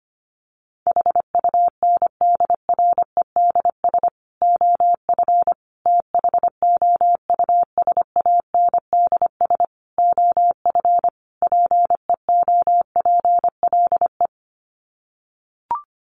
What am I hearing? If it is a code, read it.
Morse code, heard as a keyed tone, about 25 wpm, 700 Hz. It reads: HUNDREDS OF THOUSANDS OF PEOPLE
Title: Sets of 5 Words Encore2 25wpm